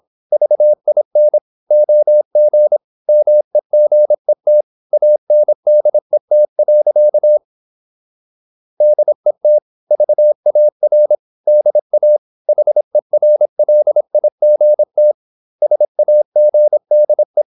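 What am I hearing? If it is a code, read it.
Morse code, heard as a keyed tone, about 26 wpm, 600 Hz. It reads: VIN OG MEGET ANDET. DET VAR DA HERLIGT SAGDE
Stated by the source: High-speed (QRQ) Morse at 26 wpm.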